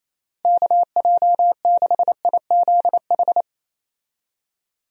Morse code, heard as a keyed tone, about 28 wpm, 700 Hz. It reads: KJ6IZH
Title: USA Call Signs - 6 Characters - Repeated 2x - 28wpm